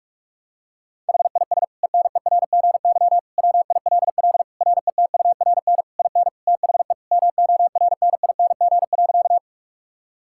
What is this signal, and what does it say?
Morse code, heard as a keyed tone, about 41 wpm, 700 Hz. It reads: HIS ENERGY WILL RETURN IN THE MORNING.
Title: Sentences from Top 1000 Words - Review Part 6 - 45wpm